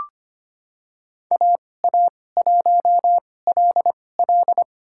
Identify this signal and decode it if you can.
Morse code, heard as a keyed tone, about 25 wpm, spaced as for 18 wpm, 700 Hz. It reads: AA1LL
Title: USA Call Signs - 5 Characters - 2x Character Spacing - 25wpm